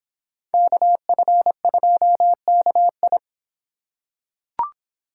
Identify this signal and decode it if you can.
Morse code, heard as a keyed tone, about 24 wpm, 700 Hz. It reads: KF2KI